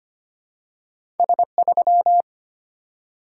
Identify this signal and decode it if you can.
Morse code, heard as a keyed tone, about 25 wpm, 700 Hz. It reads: S3
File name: Two Letter-Number Combinations - 25wpm